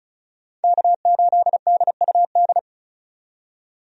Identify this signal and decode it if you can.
Morse code, heard as a keyed tone, about 36 wpm, 700 Hz. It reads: K8DUD
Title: USA Call Signs - 5 Characters - 35wpm